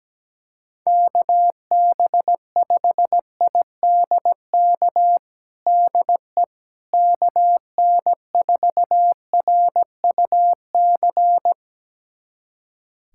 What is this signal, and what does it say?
Morse code, heard as a keyed tone, about 17 wpm, 700 Hz. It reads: KB5IDK DE KN4RUC
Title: USA Callsign DE Callsign - 6 Characters - 17wpm